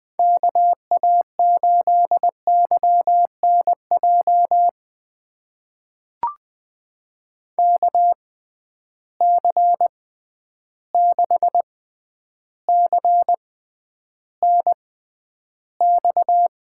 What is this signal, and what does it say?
Morse code, heard as a keyed tone, about 20 wpm, spaced as for 10 wpm, 700 Hz. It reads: KA8YNJ KC6CNX
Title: USA Call Signs - 6 Characters - 6x Character Spacing - 20wpm